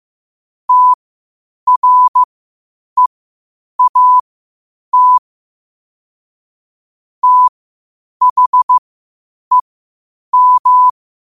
Morse code, heard as a keyed tone, about 15 wpm, spaced as for 8 wpm, 1000 Hz. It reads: TREAT THEM
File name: qotdfiles / 8wpm